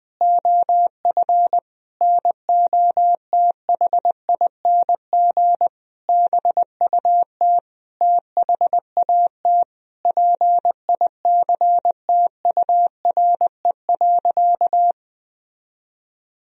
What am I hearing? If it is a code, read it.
Morse code, heard as a keyed tone, about 20 wpm, 700 Hz. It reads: OF NOTHING BUT THAT PICTURE.
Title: Sentences from Top 1000 Words - Review Part 1 - 20wpm